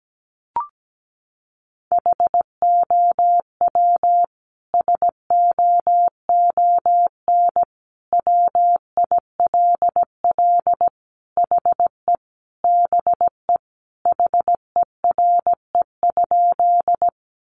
Morse code, heard as a keyed tone, about 17 wpm, 700 Hz. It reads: HOW SOON WILL HE BE HERE?
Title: Sentences from Top 400 Words 17wpm